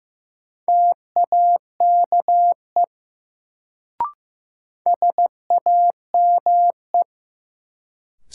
Morse code, heard as a keyed tone, about 15 wpm, 700 Hz. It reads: TAKE SAME
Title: Sets of 2 Words - New Word Drill - 15wpm